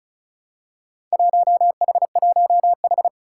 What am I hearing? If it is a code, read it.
Morse code, heard as a keyed tone, about 34 wpm, 700 Hz. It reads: 1H1H
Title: Four Challenging Letter-Number Combinations 35wpm